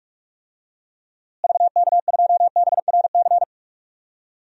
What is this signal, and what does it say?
Morse code, heard as a keyed tone, about 46 wpm, 700 Hz. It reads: VK2BRC
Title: DXCC-100 Call Signs - 6 Characters - Repeated 3x - 45wpm